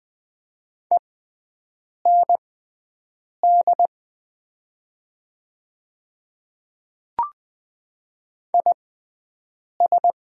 Morse code, heard as a keyed tone, about 20 wpm, spaced as for 7 wpm, 700 Hz. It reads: END IS
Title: 6x Character Spacing - Top 200 Words - 20wpm